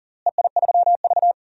Morse code, heard as a keyed tone, about 40 wpm, 700 Hz. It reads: EI3V
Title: DXCC-100 Call Signs - 4 Characters - Repeated 2x - 40wpm